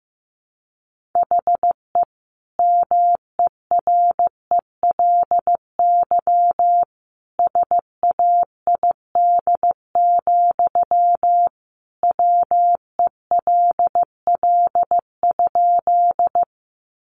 Morse code, heard as a keyed tone, about 14 wpm, 700 Hz. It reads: HE MERELY SAID, WELL?
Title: Sentences from Top 1400 Words 15wpm